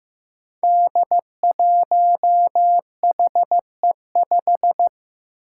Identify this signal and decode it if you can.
Morse code, heard as a keyed tone, about 15 wpm, 700 Hz. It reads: D1HE5
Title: Five Challenging Letter-Number Combinations 15wpm